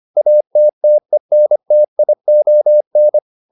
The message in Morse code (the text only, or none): ATTENTION